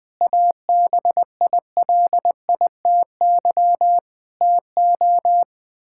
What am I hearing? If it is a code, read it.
Morse code, heard as a keyed tone, about 20 wpm, 700 Hz. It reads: ABILITY TO
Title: Build A Phrase - 3-Word Phrases - 20wpm